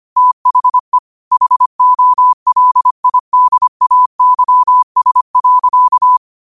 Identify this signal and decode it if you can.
Morse code, heard as a keyed tone, about 25 wpm, 1000 Hz. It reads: THE HOLIDAYS.
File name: qotdfiles / 25wpm